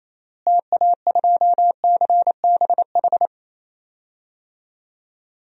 Morse code, heard as a keyed tone, about 27 wpm, 700 Hz. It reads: TA2CBH
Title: DXCC-100 Call Signs - 6 Characters - Repeated 2x - 28wpm